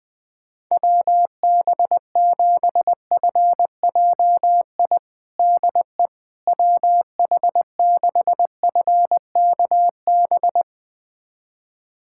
Morse code, heard as a keyed tone, about 21 wpm, 700 Hz. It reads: WB7FJI DE WH6FKB
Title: USA Callsign DE Callsign - 6 Characters - 20wpm